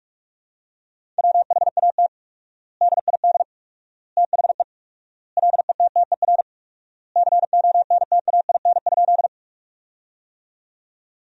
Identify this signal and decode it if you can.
Morse code, heard as a keyed tone, about 46 wpm, 700 Hz. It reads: WHAT DID THE LETTER CONTAIN?
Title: 4x Word-Spacing - Sentences from Top 500 Words - 45wpm